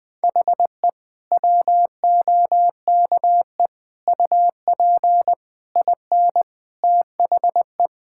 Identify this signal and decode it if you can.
Morse code, heard as a keyed tone, about 20 wpm, 700 Hz. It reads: HE WOKE UP IN THE
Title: Sentences with 6-Word Phrases - 20wpm